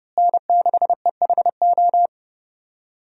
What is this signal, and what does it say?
Morse code, heard as a keyed tone, about 30 wpm, 700 Hz. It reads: N6EHO
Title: USA Call Signs - 5 Characters - Repeated 2x - 30wpm